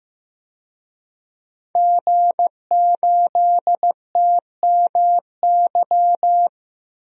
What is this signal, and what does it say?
Morse code, heard as a keyed tone, about 15 wpm, 700 Hz. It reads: G8TMY
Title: DXCC-100 Call Signs - 5 Characters - 15wpm